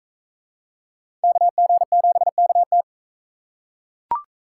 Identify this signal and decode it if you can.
Morse code, heard as a keyed tone, about 38 wpm, 700 Hz. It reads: KG7KT